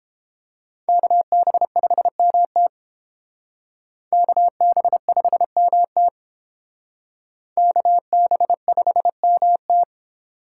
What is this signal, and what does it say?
Morse code, heard as a keyed tone, about 32 wpm, 700 Hz. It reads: KB5MT KB5MT KB5MT